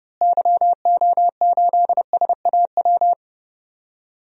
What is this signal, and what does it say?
Morse code, heard as a keyed tone, about 30 wpm, 700 Hz. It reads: YO8SAW
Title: DXCC-100 Call Signs - 6 Characters - Repeated 3x - 30wpm